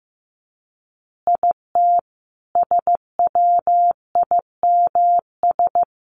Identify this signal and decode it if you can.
Morse code, heard as a keyed tone, about 15 wpm, 700 Hz. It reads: IT SWIMS
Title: Additional Easy Phrases 15wpm